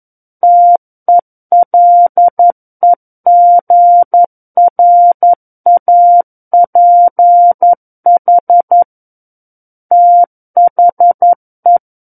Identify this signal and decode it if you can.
Morse code, heard as a keyed tone, about 11 wpm, 700 Hz. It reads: TELEGRAPH THE
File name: Book One / 02-Chapter 2 - 11 WPM